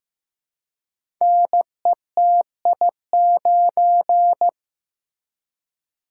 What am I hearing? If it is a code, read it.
Morse code, heard as a keyed tone, about 15 wpm, 700 Hz. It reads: NETI9
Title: Code Groups - 5 Letter:Numbers - 15wpm